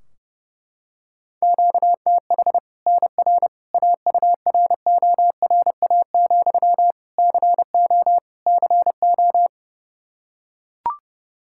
Morse code, heard as a keyed tone, about 30 wpm, 700 Hz. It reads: QTH NR AURORA, CO CO
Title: Practice QSO - Exchange 1 - Beginning - 30wpm